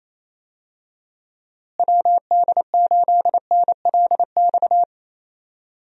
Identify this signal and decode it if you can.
Morse code, heard as a keyed tone, about 28 wpm, 700 Hz. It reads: WD8NLX
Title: USA Call Signs - 6 Characters - Repeated 2x - 28wpm